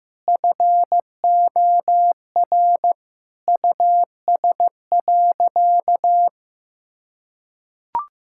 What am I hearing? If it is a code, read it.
Morse code, heard as a keyed tone, about 15 wpm, 700 Hz. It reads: FOR US.
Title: Sentences from Top 1600 Words 15wpm